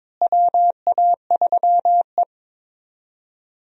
Morse code, heard as a keyed tone, about 22 wpm, 700 Hz. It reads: WA3E
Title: USA Call Signs - 4 Characters - Repeated 2x - 22wpm